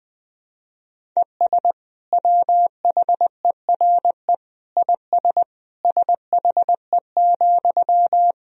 Morse code, heard as a keyed tone, about 20 wpm, 700 Hz. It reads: ES WHERE IS SHE,
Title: Sentences from Top 200 Words and Abbreviations 20wpm